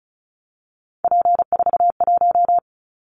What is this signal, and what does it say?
Morse code, heard as a keyed tone, about 36 wpm, 700 Hz. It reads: P41
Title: Callsign Prefixes for Top 100 DXCC - 35wpm